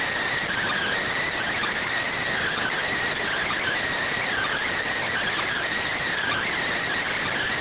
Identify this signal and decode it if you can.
Emergency Alert System (SAME) tones (header bursts), heard as harsh buzzing bursts. It is heard from 0.5 to 6.5 s.